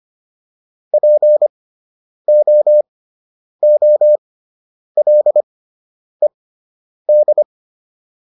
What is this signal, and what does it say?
Morse code, heard as a keyed tone, about 25 wpm, spaced as for 9 wpm, 600 Hz. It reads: POOLED